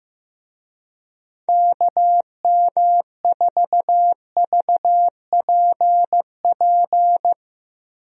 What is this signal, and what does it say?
Morse code, heard as a keyed tone, about 15 wpm, 700 Hz. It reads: KM4VPP
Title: USA Call Signs - 6 Characters - Repeated 3x - 15wpm